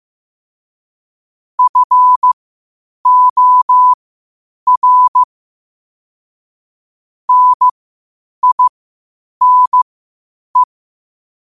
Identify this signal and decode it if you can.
Morse code, heard as a keyed tone, about 15 wpm, spaced as for 8 wpm, 1000 Hz. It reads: FOR NINE
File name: qotdfiles / 8wpm